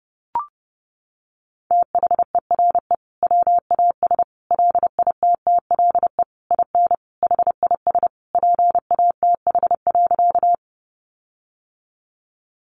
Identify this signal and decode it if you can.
Morse code, heard as a keyed tone, about 30 wpm, 700 Hz. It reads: THERE WAS LITTLE IN HIS PATH.
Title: Sentences from Top 1400 Words 30wpm